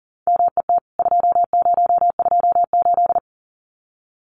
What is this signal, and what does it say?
Morse code, heard as a keyed tone, about 39 wpm, 700 Hz. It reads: MET 2028
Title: CWT Exchange 40wpm